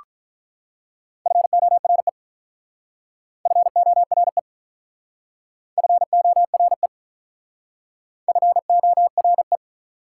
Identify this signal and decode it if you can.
Morse code, heard as a keyed tone, about 44 wpm, 700 Hz. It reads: FORE FORE FORE FORE